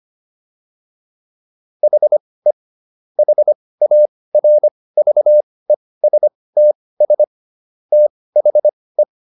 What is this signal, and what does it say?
Morse code, heard as a keyed tone, about 25 wpm, spaced as for 18 wpm, 600 Hz. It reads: HE HARVESTS THE